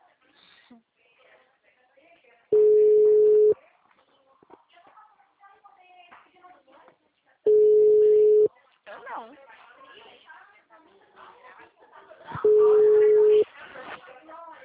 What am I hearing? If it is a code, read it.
Ringback tone (Europe, 425 Hz), with speech.
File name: out- -2012- - - .2